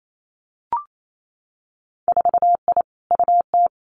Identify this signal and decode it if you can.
Morse code, heard as a keyed tone, about 28 wpm, 700 Hz. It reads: 4I UT
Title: Winter Field Day Exchanges Encore 28wpm